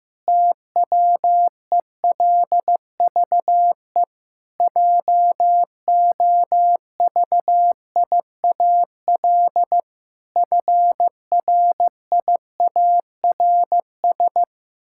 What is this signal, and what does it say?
Morse code, heard as a keyed tone, about 15 wpm, 700 Hz. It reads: TWELVE JOVIAL FRIARS
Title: Pangram Challenge v2 - 15wpm